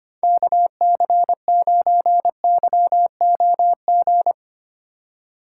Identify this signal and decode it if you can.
Morse code, heard as a keyed tone, about 25 wpm, 700 Hz. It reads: KC9YOG
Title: USA Call Signs - 6 Characters - 4x Character Spacing - 25wpm